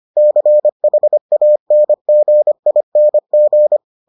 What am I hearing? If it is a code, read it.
Morse code, heard as a keyed tone, about 25 wpm, 600 Hz. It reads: CHANGING